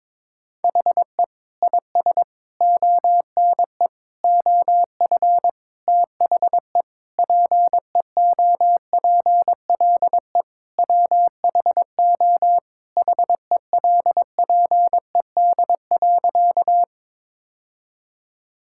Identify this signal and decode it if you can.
Morse code, heard as a keyed tone, about 22 wpm, 700 Hz. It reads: HE IS ONE OF THE PEOPLE WHO HELPED.
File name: practice / 22wpm / Sentences with 4-Word Phrases - 22wpm